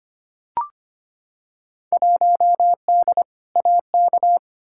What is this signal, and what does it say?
Morse code, heard as a keyed tone, about 25 wpm, 700 Hz. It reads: 1D AK
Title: Field Day Exchanges - 25wpm